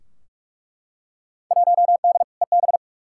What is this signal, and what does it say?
Morse code, heard as a keyed tone, about 41 wpm, 700 Hz. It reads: 1D EB
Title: Field Day Exchanges - 45wpm